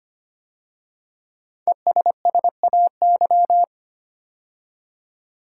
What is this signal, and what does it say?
Morse code, heard as a keyed tone, about 25 wpm, 700 Hz. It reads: ESSAY
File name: 5 Letter Words - First 3 Letters Given - 25wpm